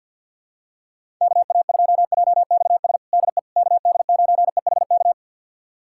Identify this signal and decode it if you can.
Morse code, heard as a keyed tone, about 47 wpm, 700 Hz. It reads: KA2JXS DE KD9EHK
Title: USA Callsign DE Callsign - 6 Characters - 50wpm